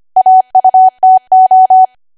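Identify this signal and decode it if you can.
Morse code, heard as a keyed tone, about 25 wpm, 750 Hz. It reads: AUTO
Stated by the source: ARRL 25 WPM code practice archive.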